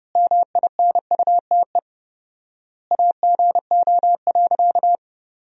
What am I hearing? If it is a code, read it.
Morse code, heard as a keyed tone, about 30 wpm, 700 Hz. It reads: MINUTE AGO.
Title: 4x Word-Spacing - Sentences from Top 500 Words - 30wpm